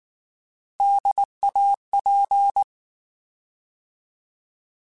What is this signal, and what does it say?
Morse code, heard as a keyed tone, about 19 wpm, 800 Hz. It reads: DAP
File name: Alphabetical / Alphabetical mp3s / Ds @18